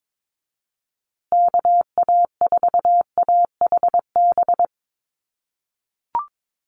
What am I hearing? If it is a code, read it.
Morse code, heard as a keyed tone, about 22 wpm, 700 Hz. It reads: KA4AHB
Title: USA Call Signs Repeated 2x 22wpm